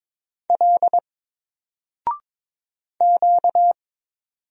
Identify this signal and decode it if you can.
Morse code, heard as a keyed tone, about 22 wpm, 700 Hz. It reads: L Q
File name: Single-Letter-Number-22wpm